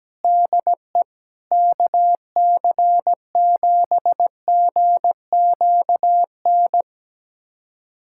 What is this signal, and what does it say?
Morse code, heard as a keyed tone, about 17 wpm, 700 Hz. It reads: DE KC7GQN
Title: USA Callsign DE Callsign 17wpm